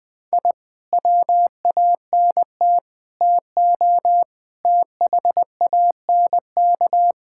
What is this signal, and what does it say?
Morse code, heard as a keyed tone, about 20 wpm, 700 Hz. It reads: I WANT TO THANK
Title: Sentences with 7-Word Phrases - 20wpm